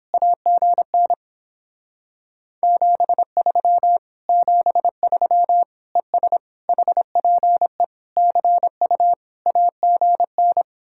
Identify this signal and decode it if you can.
Morse code, heard as a keyed tone, about 31 wpm, 700 Hz. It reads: AGN 73 73 ES HPE CU AGN